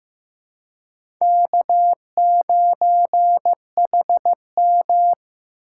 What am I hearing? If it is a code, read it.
Morse code, heard as a keyed tone, about 15 wpm, 700 Hz. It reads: K9HM
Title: USA Call Signs - 3x Character Spacing - 15wpm